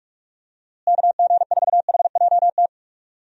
Morse code, heard as a keyed tone, about 43 wpm, 700 Hz. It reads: KG4HJT
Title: USA Call Signs - 6 Characters - Repeated 3x - 45wpm